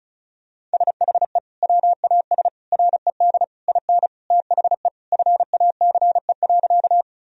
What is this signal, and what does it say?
Morse code, heard as a keyed tone, about 36 wpm, 700 Hz. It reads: SHE WAS RED IN THE FACE.
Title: Sentences from Top 1000 Words - Review Part 2 - 35wpm